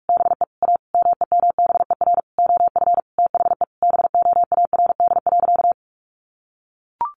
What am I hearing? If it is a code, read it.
Morse code, heard as a keyed tone, about 44 wpm, 700 Hz. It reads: BE A MEMBER OF THE BOARD.
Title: Sentences from Top 1100 Words 45wpm